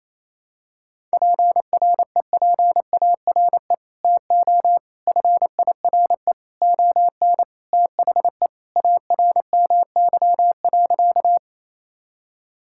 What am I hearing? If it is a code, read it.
Morse code, heard as a keyed tone, about 28 wpm, 700 Hz. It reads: PREPARE TO FIRE ON THE ARMY.
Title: Sentences from Top 1000 Words - Review Part 6 - 28wpm